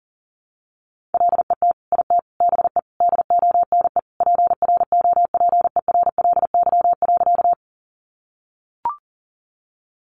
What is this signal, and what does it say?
Morse code, heard as a keyed tone, about 40 wpm, 700 Hz. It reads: LET IT BE DONE PROPERLY.
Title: Sentences from Top 1100 Words 40wpm